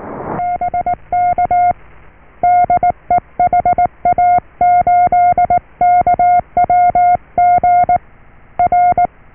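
Morse code, heard as a keyed tone, about 19 wpm, 710 Hz. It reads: BK DE HA8KWG R
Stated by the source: Morse QSO on 14 MHz with a German station, received at RST 579.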